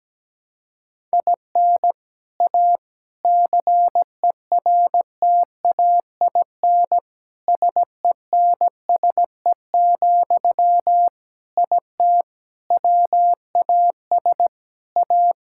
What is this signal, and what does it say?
Morse code, heard as a keyed tone, about 17 wpm, 700 Hz. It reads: IN A CERTAIN SENSE, IT WAS A
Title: Sentences from Top 1000 Words - Review Part 6 - 17wpm